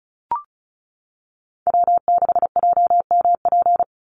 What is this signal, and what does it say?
Morse code, heard as a keyed tone, about 34 wpm, 700 Hz. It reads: W6JMP
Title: NA SOTA Chaser Callsigns 35wpm